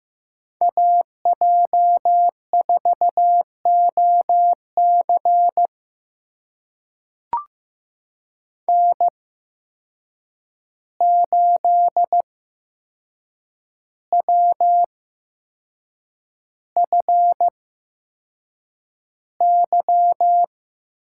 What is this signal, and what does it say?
Morse code, heard as a keyed tone, about 15 wpm, spaced as for 6 wpm, 700 Hz. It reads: AJ4OC N8WFY
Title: USA Call Signs - 5 Characters - 8x Character Spacing - 15wpm